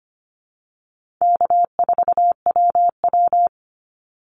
Morse code, heard as a keyed tone, about 25 wpm, 700 Hz. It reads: K4WW
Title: Easy Call Signs - 25wpm